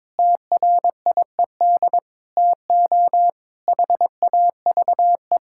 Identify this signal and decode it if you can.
Morse code, heard as a keyed tone, about 22 wpm, 700 Hz. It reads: TRIED TO HAVE
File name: practice / 22wpm / Sentences with 8-Word Phrases - 22wpm